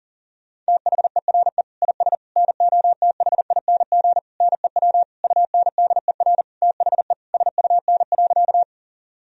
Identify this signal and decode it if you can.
Morse code, heard as a keyed tone, about 37 wpm, 700 Hz. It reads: THERE IS NOTHING NEW UNDER THE SUN.
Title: Sentences from Top 1000 Words - Review Part 1 - 40wpm